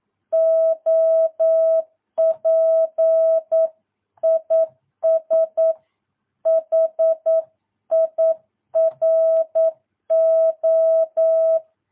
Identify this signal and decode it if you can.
Morse code, heard as a keyed tone, about 9 wpm, 650 Hz. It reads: OP IS HIRO
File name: jh8cbh / 2022cw